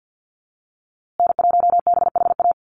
Morse code, heard as a keyed tone, about 50 wpm, 700 Hz. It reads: N1BHA
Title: USA Call Signs Repeated 3x 50wpm